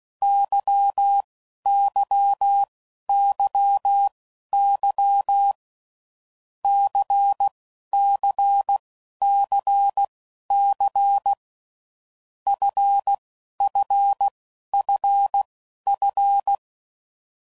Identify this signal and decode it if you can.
Morse code, heard as a keyed tone, about 16 wpm, spaced as for 11 wpm, 800 Hz. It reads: YYYY CCCC FFFF
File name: Groups / Basic Letter Groups / LFCYQ